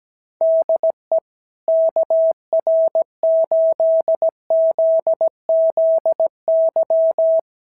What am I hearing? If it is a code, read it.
Morse code, heard as a keyed tone, about 17 wpm, 660 Hz. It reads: DE KR8ZZY
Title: SOTA QSO Practice 17wpm